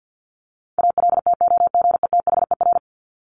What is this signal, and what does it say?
Morse code, heard as a keyed tone, about 50 wpm, 700 Hz. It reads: ALTOGETHER
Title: Top 1501-1750 Words 50wpm